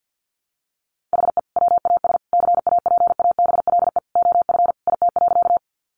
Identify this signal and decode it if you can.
Morse code, heard as a keyed tone, about 50 wpm, 700 Hz. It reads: HE WAS CAPABLE OF IT.
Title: Sentences from Top 1200 Words 50wpm